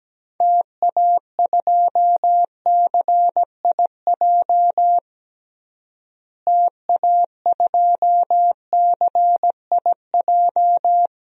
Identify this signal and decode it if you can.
Morse code, heard as a keyed tone, about 17 wpm, 700 Hz. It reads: TA2CIJ TA2CIJ
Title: DXCC-100 Call Signs - 6 Characters - Repeated 3x - 17wpm